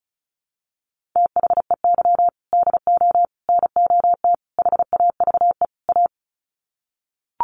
Morse code, heard as a keyed tone, about 34 wpm, 700 Hz. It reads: THEY DO NOT HAVE A
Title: Sets of 5 Words 35wpm